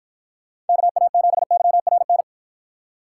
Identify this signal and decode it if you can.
Morse code, heard as a keyed tone, about 53 wpm, 700 Hz. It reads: KA7YRN